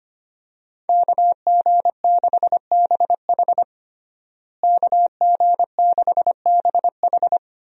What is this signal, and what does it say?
Morse code, heard as a keyed tone, about 25 wpm, 700 Hz. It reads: KG6BH KG6BH
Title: USA Call Signs - 5 Characters - Repeated 2x - 25wpm